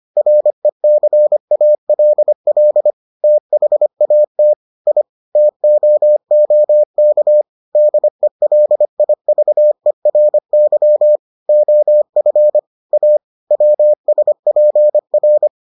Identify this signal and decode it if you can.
Morse code, heard as a keyed tone, about 25 wpm, 600 Hz. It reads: RECALL THAT I TOOK DELIVERY OF A WSPR